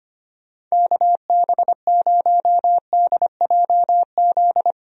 Morse code, heard as a keyed tone, about 25 wpm, 700 Hz. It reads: KB0DJZ
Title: USA Call Signs - 6 Characters - 25wpm